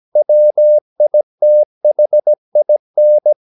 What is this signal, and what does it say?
Morse code, heard as a keyed tone, about 17 wpm, 600 Hz. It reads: WITHIN
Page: Romanen Never af Ken Follet på engelsk.